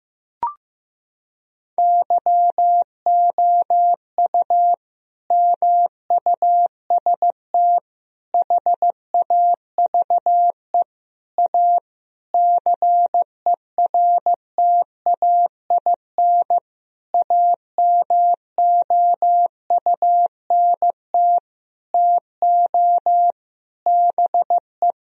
Morse code, heard as a keyed tone, about 15 wpm, 700 Hz. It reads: YOU MUST HAVE A CERTAIN AMOUNT TO BE
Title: Sentences from Top 1000 Words - Review Part 4 - 15wpm